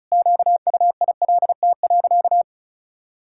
Morse code, heard as a keyed tone, about 36 wpm, 700 Hz. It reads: QUILT.
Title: Pangram Challenge v2 - 35wpm